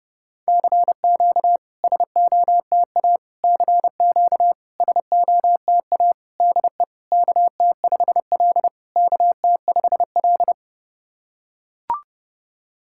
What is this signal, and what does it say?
Morse code, heard as a keyed tone, about 28 wpm, 700 Hz. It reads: CQ SOTA CQ SOTA DE KT5L KT5L
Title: SOTA QSO Practice 30wpm